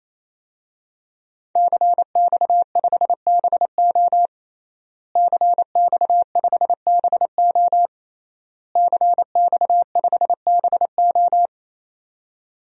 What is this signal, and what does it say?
Morse code, heard as a keyed tone, about 28 wpm, 700 Hz. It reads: CX5BO CX5BO CX5BO
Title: DXCC-100 Call Signs - 5 Characters - Repeated 3x - 28wpm